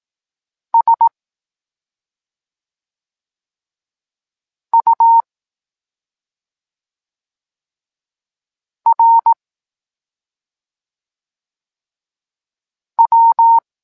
Morse code, heard as a keyed tone, about 18 wpm, 920 Hz. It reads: S U R W
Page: Tekst 23.3 – Ciąg dalszy doskonalenia odbioru poznanych liter